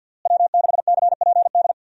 Morse code, heard as a keyed tone, about 51 wpm, 700 Hz. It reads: W6CPD
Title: USA Call Signs - 5 Characters - Repeated 3x - 50wpm